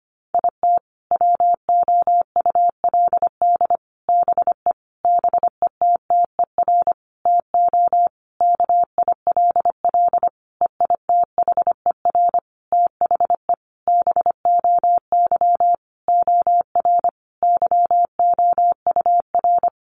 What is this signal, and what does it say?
Morse code, heard as a keyed tone, about 25 wpm, 700 Hz. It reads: IT WOULD BE BETTER TO KILL EITHER THE BOY OR YOUR
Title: Sentences from Top 1500 Words 25wpm